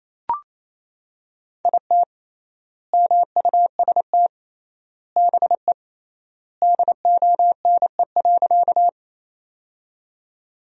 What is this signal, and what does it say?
Morse code, heard as a keyed tone, about 28 wpm, 700 Hz. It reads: IT MUST BE DONE.
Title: 3x Word-Spacing - Sentences from Top 500 Words - 28wpm